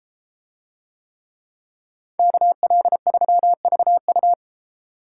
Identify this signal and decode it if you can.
Morse code, heard as a keyed tone, about 32 wpm, 700 Hz. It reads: KL3VU